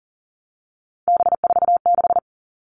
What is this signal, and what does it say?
Morse code, heard as a keyed tone, about 40 wpm, 700 Hz. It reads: B46
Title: All Cut Numbers - 3-Digit 40wpm